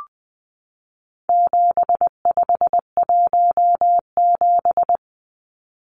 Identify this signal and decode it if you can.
Morse code, heard as a keyed tone, about 19 wpm, 700 Hz. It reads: 7517
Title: Numbers - 4-Digit 20wpm